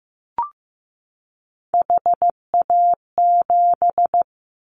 Morse code, heard as a keyed tone, about 15 wpm, 700 Hz. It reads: HA7
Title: Callsign Prefixes for Top 25 DXCC - 15wpm